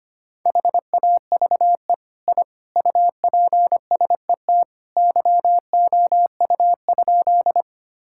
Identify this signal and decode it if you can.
Morse code, heard as a keyed tone, about 25 wpm, 700 Hz. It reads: HAVE I UPSET YOU?
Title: Sentences from Top 1000 Words - Review Part 5 - 25wpm